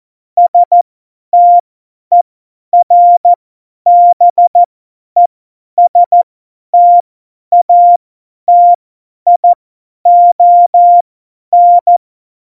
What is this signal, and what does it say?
Morse code, heard as a keyed tone, about 14 wpm, spaced as for 10 wpm, 700 Hz. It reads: STERBESTATION
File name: Morsekurs / Witze / Witz 15. September 2020, 50 F70